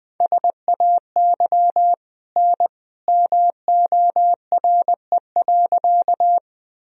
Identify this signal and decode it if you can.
Morse code, heard as a keyed tone, about 21 wpm, 700 Hz. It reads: SAY N MORE.
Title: Sentences from Top 200 Words and Abbreviations 20wpm